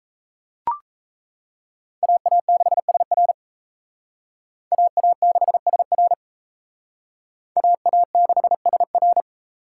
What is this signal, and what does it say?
Morse code, heard as a keyed tone, about 35 wpm, 700 Hz. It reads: AA6SR AA6SR AA6SR